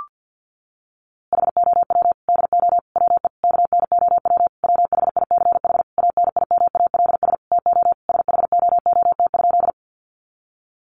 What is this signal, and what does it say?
Morse code, heard as a keyed tone, about 50 wpm, 700 Hz. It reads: HOW DO WE KNOW WHICH ANIMALS TO SHOOT?
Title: A Sound Of Thunder 50wpm